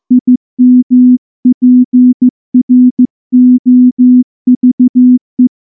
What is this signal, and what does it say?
Morse code, heard as a keyed tone, about 14 wpm, 250 Hz. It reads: IMPROVE
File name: morse-training-15WPM